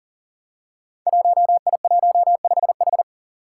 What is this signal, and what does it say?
Morse code, heard as a keyed tone, about 40 wpm, 700 Hz. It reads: 1I15H
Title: Five Challenging Letter-Number Combinations 40wpm